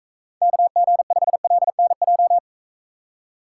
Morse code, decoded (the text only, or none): KG5LNJ